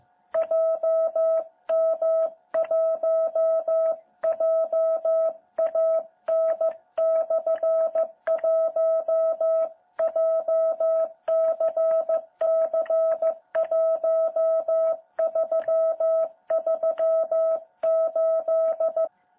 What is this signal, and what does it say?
Morse code, heard as a keyed tone, about 15 wpm, 660 Hz. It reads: JM1JAN/1 JCC1338
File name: jh8cbh / 2021cw